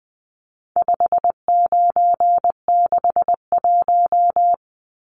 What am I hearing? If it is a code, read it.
Morse code, heard as a keyed tone, about 20 wpm, 700 Hz. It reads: 5961
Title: Numbers - 4-Digit 20wpm